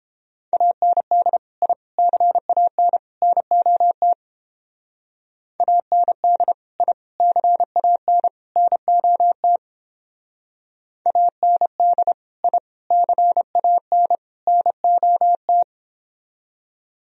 Morse code, heard as a keyed tone, about 29 wpm, 700 Hz. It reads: AND I CAN NOT AND I CAN NOT AND I CAN NOT